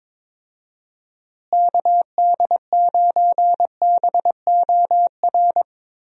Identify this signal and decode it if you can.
Morse code, heard as a keyed tone, about 22 wpm, 700 Hz. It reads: KD9BOR
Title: USA Call Signs - 6 Characters - Repeated 3x - 22wpm